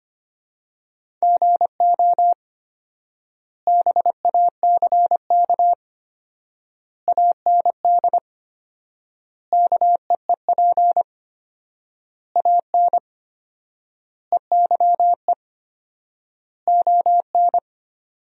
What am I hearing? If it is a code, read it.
Morse code, heard as a keyed tone, about 25 wpm, 700 Hz. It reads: GO BACK AND KEEP AN EYE ON
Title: 4x Word-Spacing - Sentences from Top 300 Words - 25wpm